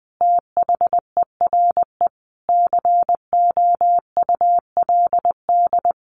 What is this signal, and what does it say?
Morse code, heard as a keyed tone, about 20 wpm, 700 Hz. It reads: THERE COULD
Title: A Sound Of Thunder 20wpm